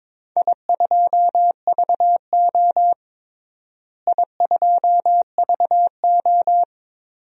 Morse code, heard as a keyed tone, about 22 wpm, 700 Hz. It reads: I2VO I2VO
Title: DXCC-100 Call Signs - 4 Characters - Repeated 2x - 22wpm